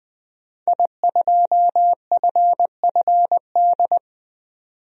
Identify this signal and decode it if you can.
Morse code, heard as a keyed tone, about 20 wpm, 700 Hz. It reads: I2FFD
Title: DXCC-100 Call Signs - 5 Characters - 20wpm